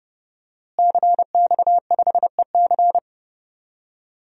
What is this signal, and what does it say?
Morse code, heard as a keyed tone, about 30 wpm, 700 Hz. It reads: CX5EC
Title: DXCC-100 Call Signs - 5 Characters - Repeated 3x - 30wpm